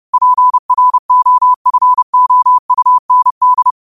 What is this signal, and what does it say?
Morse code, heard as a keyed tone, about 30 wpm, 1000 Hz. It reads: PROFOUND